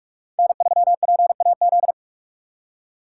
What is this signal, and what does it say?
Morse code, heard as a keyed tone, about 45 wpm, 700 Hz. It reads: N3PAZ
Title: USA Call Signs - 5 Characters - 2x Character Spacing - 45wpm